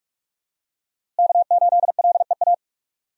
Morse code, heard as a keyed tone, about 46 wpm, 700 Hz. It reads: K8LEA
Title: USA Call Signs - 5 Characters - Repeated 3x - 45wpm